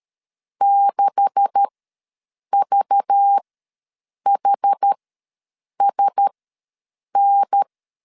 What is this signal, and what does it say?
Morse code, heard as a keyed tone, about 13 wpm, spaced as for 7 wpm, 800 Hz. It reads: 6VHSN
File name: LEZIONI / 9 286 ws 12wpm 3351